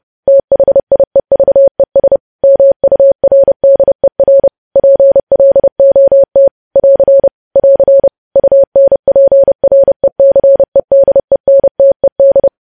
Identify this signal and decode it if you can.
Morse code, heard as a keyed tone, about 30 wpm, 550 Hz. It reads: THIEVES MURDER PLOT <AR> <AR> UNPRECEDENTED
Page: news headlines in morse code at 30 wpm